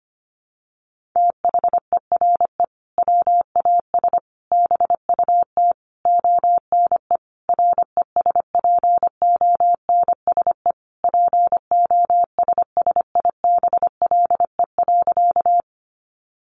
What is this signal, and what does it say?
Morse code, heard as a keyed tone, about 25 wpm, 700 Hz. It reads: THERE WAS BUT ONE RESPONSE POSSIBLE.
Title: Sentences from Top 1400 Words 25wpm